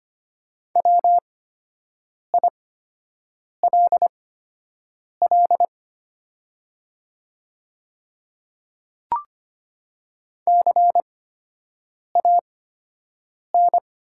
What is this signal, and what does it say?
Morse code, heard as a keyed tone, about 25 wpm, spaced as for 7 wpm, 700 Hz. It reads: WILL CAN
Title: 8x Character Spacing - Top 100 Words - 25wpm